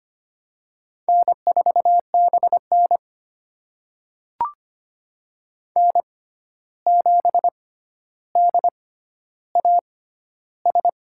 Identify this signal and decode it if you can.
Morse code, heard as a keyed tone, about 25 wpm, spaced as for 12 wpm, 700 Hz. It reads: N4BN N7DAS
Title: USA Call Signs - 6x Character Spacing - 25wpm